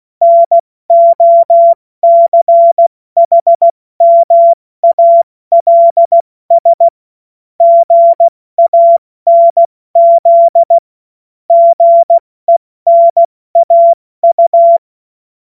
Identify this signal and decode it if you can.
Morse code, heard as a keyed tone, about 16 wpm, 680 Hz. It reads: NOCHMALS GANZ GENAU
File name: Morsekurs / Witze / Witz 07. Dezember 2021, 70 F80